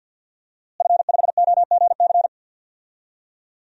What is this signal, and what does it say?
Morse code, heard as a keyed tone, about 51 wpm, 700 Hz. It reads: F5OGC
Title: DXCC-100 Call Signs - 5 Characters - Repeated 2x - 50wpm